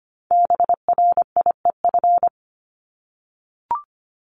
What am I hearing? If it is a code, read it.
Morse code, heard as a keyed tone, about 25 wpm, 700 Hz. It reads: BRIEF
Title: LY Suffix - Top 100 Words with Base Words 25wpm